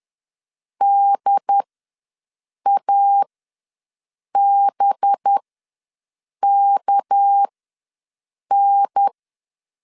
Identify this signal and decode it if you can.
Morse code, heard as a keyed tone, about 10 wpm, spaced as for 6 wpm, 800 Hz. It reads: DABKN
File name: LEZIONI / 7 GF9Z / 7 GF9 ws 10wpm 4034